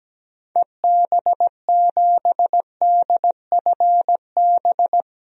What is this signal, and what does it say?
Morse code, heard as a keyed tone, about 17 wpm, 700 Hz. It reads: EB7DFB
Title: DXCC-100 Call Signs - 6 Characters - 17wpm